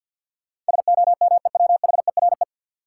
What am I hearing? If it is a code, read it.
Morse code, heard as a keyed tone, about 49 wpm, 700 Hz. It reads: SOMEWHERE
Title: Sentences from Top 1400 Words - New Word Drill Alt - 50wpm